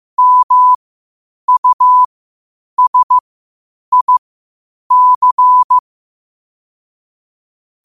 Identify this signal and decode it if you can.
Morse code, heard as a keyed tone, about 15 wpm, spaced as for 8 wpm, 1000 Hz. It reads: MUSIC
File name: 8wpm